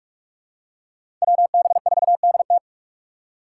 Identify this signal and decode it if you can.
Morse code, heard as a keyed tone, about 46 wpm, 700 Hz. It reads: WB4DT
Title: USA Call Signs - 3x Character Spacing - 45wpm